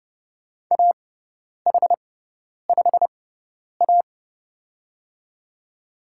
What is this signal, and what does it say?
Morse code, heard as a keyed tone, about 30 wpm, spaced as for 10 wpm, 700 Hz. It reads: AH5A
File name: Lesson 18g - Letter m - Call Signs 30wpm